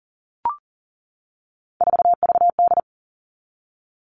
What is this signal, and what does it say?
Morse code, heard as a keyed tone, about 40 wpm, 700 Hz. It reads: 4VD
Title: All Cut Numbers - 3-Digit 40wpm